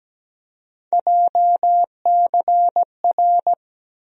Morse code, heard as a keyed tone, about 17 wpm, 700 Hz. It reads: JCR
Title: USA Call Sign Postfixes - 3 Letters 17wpm